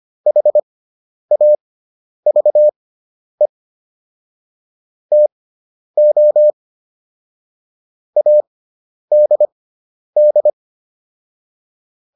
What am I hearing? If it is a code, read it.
Morse code, heard as a keyed tone, about 25 wpm, spaced as for 10 wpm, 600 Hz. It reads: HAVE TO ADD